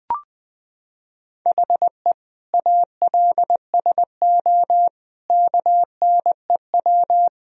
Morse code, heard as a keyed tone, about 20 wpm, 700 Hz. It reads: HE ALSO KNEW
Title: Sentences from Top 1000 Words - Review Part 1 - 20wpm